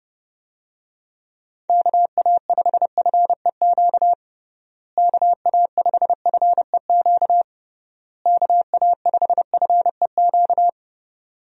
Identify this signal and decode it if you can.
Morse code, heard as a keyed tone, about 30 wpm, 700 Hz. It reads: KA5FEQ KA5FEQ KA5FEQ
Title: USA Call Signs - 6 Characters - Repeated 3x - 30wpm